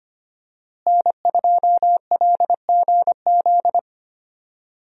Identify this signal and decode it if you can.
Morse code, heard as a keyed tone, about 25 wpm, 700 Hz. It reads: N2LGZ
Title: USA Call Signs - 5 Characters - Repeated 3x - 25wpm